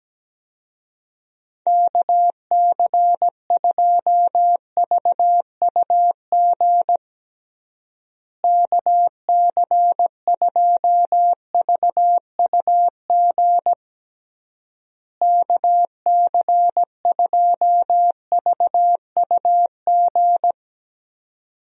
Morse code, heard as a keyed tone, about 17 wpm, 700 Hz. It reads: KC2VUG KC2VUG KC2VUG
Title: USA Call Signs - 6 Characters - Repeated 3x - 17wpm